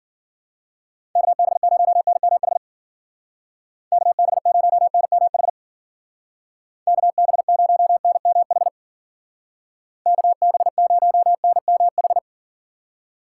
Morse code, heard as a keyed tone, about 50 wpm, 700 Hz. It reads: KB0NMH KB0NMH KB0NMH KB0NMH
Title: Speed Racing - USA 2x3 Call Signs 40wpm